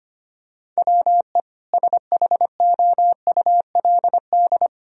Morse code, heard as a keyed tone, about 25 wpm, 700 Hz. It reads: WE SHOULD
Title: Sentences from Top 1000 Words - Review Part 3 - 25wpm